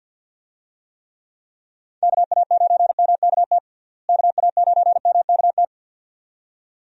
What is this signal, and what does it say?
Morse code, heard as a keyed tone, about 48 wpm, 700 Hz. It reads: KA9MKT KA9MKT
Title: USA Call Signs - 6 Characters - Repeated 2x - 50wpm